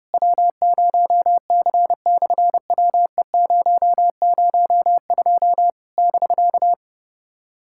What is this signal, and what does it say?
Morse code, heard as a keyed tone, about 30 wpm, 700 Hz. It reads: W0C/WE002 <BK>
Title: SOTA QSO Practice 30wpm